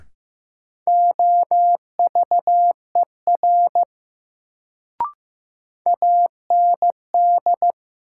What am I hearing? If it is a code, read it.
Morse code, heard as a keyed tone, about 15 wpm, 700 Hz. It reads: OVER AND
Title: Sentences from Top 100 Words - New Word Drill - 15wpm